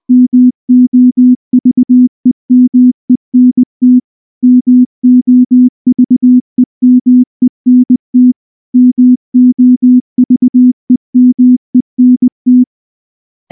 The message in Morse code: MOVEMENT MOVEMENT MOVEMENT